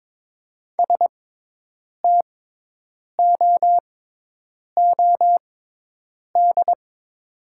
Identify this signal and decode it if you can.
Morse code, heard as a keyed tone, about 21 wpm, spaced as for 8 wpm, 700 Hz. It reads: STOOD
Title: 6x Character Spacing - Top 500 Words - 22wpm